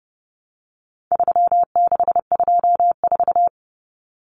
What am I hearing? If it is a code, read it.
Morse code, heard as a keyed tone, about 28 wpm, 700 Hz. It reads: 3624
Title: Common Cut Numbers - 4-Digit 30wpm